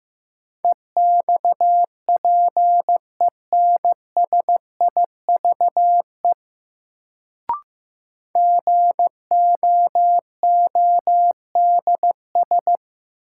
Morse code, heard as a keyed tone, about 15 wpm, 700 Hz. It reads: EXPENSIVE GOODS
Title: Sentences from Top 1000 Words - Review Part 6 - New Word Drill - 15wpm